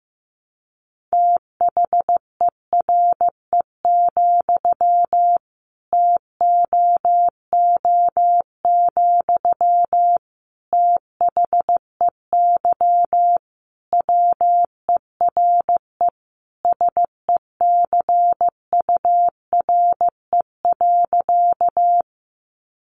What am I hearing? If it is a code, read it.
Morse code, heard as a keyed tone, about 15 wpm, 700 Hz. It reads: THERE, TOO, THEY WERE SECURE.
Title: Sentences from Top 1200 Words 15wpm